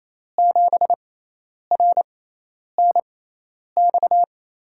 Morse code, heard as a keyed tone, about 28 wpm, spaced as for 10 wpm, 700 Hz. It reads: 7RNX
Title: USA Call Signs - 6x Character Spacing - 28wpm